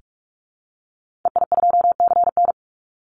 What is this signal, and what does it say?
Morse code, heard as a keyed tone, about 44 wpm, 700 Hz. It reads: EI2CN
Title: Call Signs Repeated 3x - 45wpm